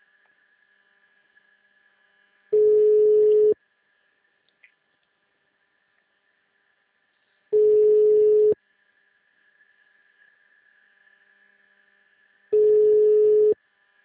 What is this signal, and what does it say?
Ringback tone (Europe, 425 Hz).
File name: out- -2001- - - .2498